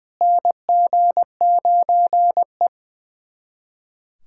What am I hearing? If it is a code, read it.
Morse code, heard as a keyed tone, about 20 wpm, 700 Hz. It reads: NG9E
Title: USA Callsign DE Callsign - 4 Characters - 20wpm